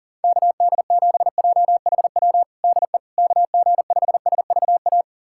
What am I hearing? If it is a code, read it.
Morse code, heard as a keyed tone, about 37 wpm, 700 Hz. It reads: KD7JHW DE KG5SVA
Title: USA Callsign DE Callsign - 6 Characters - 40wpm